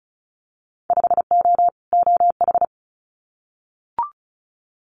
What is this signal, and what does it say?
Morse code, heard as a keyed tone, about 34 wpm, 700 Hz. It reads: 5O OH
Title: Winter Field Day Exchanges Encore 35wpm